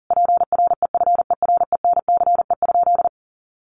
Morse code, heard as a keyed tone, about 40 wpm, 700 Hz. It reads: PREFERENCE?
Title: Sentences from Top 1500 Words 40wpm